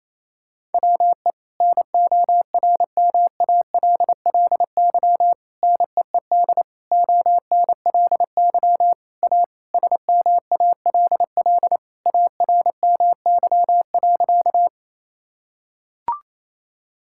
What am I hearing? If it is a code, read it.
Morse code, heard as a keyed tone, about 27 wpm, 700 Hz. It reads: WE NORMALLY NEED ONLY A SMALL ARMY.
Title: Sentences from Top 1000 Words - Review Part 6 - 28wpm